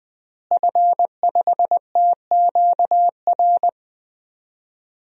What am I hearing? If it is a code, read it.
Morse code, heard as a keyed tone, about 20 wpm, 700 Hz. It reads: F5TQR
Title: DXCC-100 Call Signs - 5 Characters - Repeated 3x - 20wpm